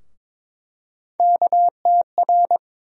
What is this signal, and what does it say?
Morse code, heard as a keyed tone, about 22 wpm, 700 Hz. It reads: KTR
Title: USA Call Sign Postfixes - 3 Letters 22wpm